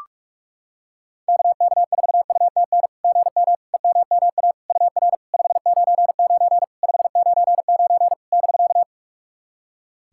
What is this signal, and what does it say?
Morse code, heard as a keyed tone, about 43 wpm, 690 Hz. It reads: KK4UTN GM EMMA UR 599 599 <BK>
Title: SOTA QSO Practice 45wpm